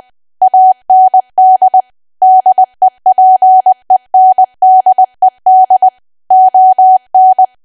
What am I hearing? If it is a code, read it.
Morse code, heard as a keyed tone, about 20 wpm, 750 Hz. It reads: AND DEPENDED ON
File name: cwa / W1AW-Level3 / 20WPM